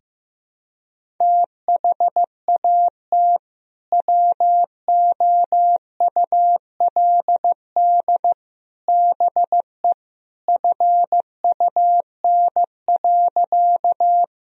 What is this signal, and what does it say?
Morse code, heard as a keyed tone, about 15 wpm, 700 Hz. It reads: THAT WOULD BE FUN.
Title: Sentences from Top 1000 Words - Review Part 4 - 15wpm